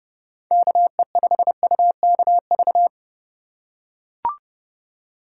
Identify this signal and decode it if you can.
Morse code, heard as a keyed tone, about 30 wpm, 700 Hz. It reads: KE5UKV